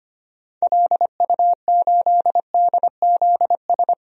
Morse code, heard as a keyed tone, about 25 wpm, 700 Hz. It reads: LU8DZS
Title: DXCC-100 Call Signs - 6 Characters - 25wpm